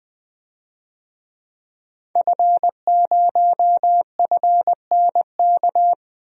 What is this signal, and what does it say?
Morse code, heard as a keyed tone, about 20 wpm, 700 Hz. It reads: F0FNK
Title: DXCC-100 Call Signs - 5 Characters - 20wpm